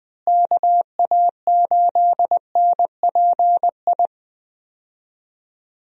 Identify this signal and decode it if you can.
Morse code, heard as a keyed tone, about 21 wpm, 700 Hz. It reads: KA8NPI
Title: USA Call Signs - 6 Characters - Repeated 2x - 20wpm